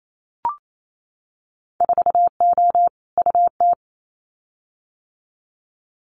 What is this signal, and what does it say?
Morse code, heard as a keyed tone, about 28 wpm, 700 Hz. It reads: 4O UT
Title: Winter Field Day Exchanges Encore 28wpm